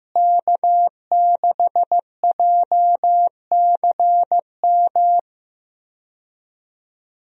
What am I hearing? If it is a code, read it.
Morse code, heard as a keyed tone, about 15 wpm, 700 Hz. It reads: K6JCM
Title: USA Call Signs - 5 Characters - Repeated 2x - 15wpm